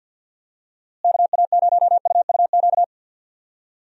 Morse code, heard as a keyed tone, about 52 wpm, 700 Hz. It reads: KA0UUQ
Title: USA Call Signs - 6 Characters - Repeated 3x - 50wpm